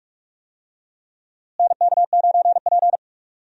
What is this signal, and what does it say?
Morse code, heard as a keyed tone, about 46 wpm, 700 Hz. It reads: NK9P
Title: USA Call Signs - 4 Characters - Repeated 2x - 45wpm